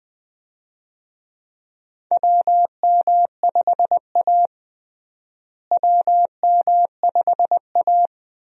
Morse code, heard as a keyed tone, about 20 wpm, 700 Hz. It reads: WM5A WM5A
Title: USA Call Signs - 4 Characters - Repeated 3x - 20wpm